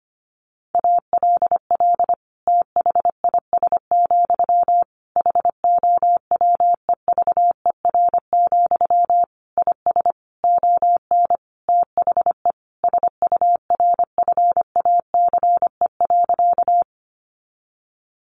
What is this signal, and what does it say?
Morse code, heard as a keyed tone, about 25 wpm, 700 Hz. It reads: ALL THIS, HOWEVER, IS ON THE SURFACE.
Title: Sentences from Top 600 Words 25wpm